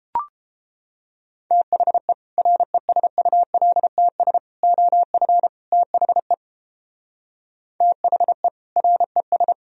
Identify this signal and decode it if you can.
Morse code, heard as a keyed tone, about 33 wpm, 700 Hz. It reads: THE RESULTS OF THE THE RES